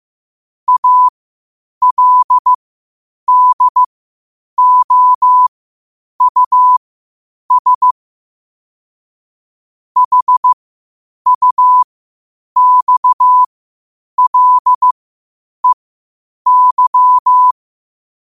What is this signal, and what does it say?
Morse code, heard as a keyed tone, about 15 wpm, spaced as for 8 wpm, 1000 Hz. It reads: ALDOUS HUXLEY
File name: qotdfiles / 8wpm